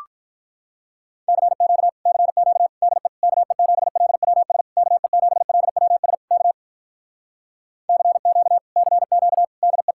Morse code, heard as a keyed tone, about 46 wpm, 700 Hz. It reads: CQ CQ DE KE7LWS KE7LWS K CQ CQ DE